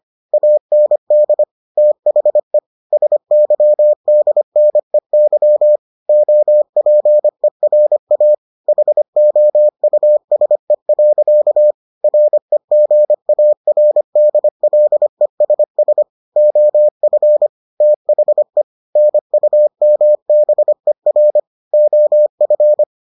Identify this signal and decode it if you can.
Morse code, heard as a keyed tone, about 25 wpm, 600 Hz. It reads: AND THE SYDNEY OPERA HOUSE. REGARDLESS OF THE NUMBER OF